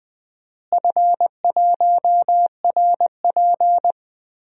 Morse code, heard as a keyed tone, about 20 wpm, 700 Hz. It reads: F1RP
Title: DXCC-100 Call Signs - 4 Characters - Repeated 3x - 20wpm